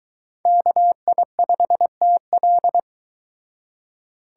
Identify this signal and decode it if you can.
Morse code, heard as a keyed tone, about 22 wpm, 700 Hz. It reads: KI5TL